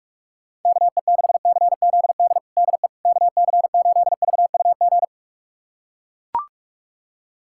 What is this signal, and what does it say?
Morse code, heard as a keyed tone, about 41 wpm, 690 Hz. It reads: KE6CZD DE KC8VUG
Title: USA Callsign DE Callsign - 6 Characters - 45wpm